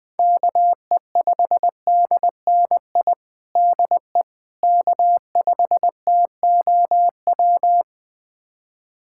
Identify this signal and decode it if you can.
Morse code, heard as a keyed tone, about 20 wpm, 700 Hz. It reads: KE5DNI DE K5TOW
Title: USA Callsign DE Callsign 20wpm